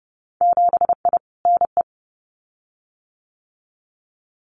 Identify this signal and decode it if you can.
Morse code, heard as a keyed tone, about 30 wpm, 700 Hz. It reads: 7I NE
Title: Winter Field Day Exchanges Encore 30wpm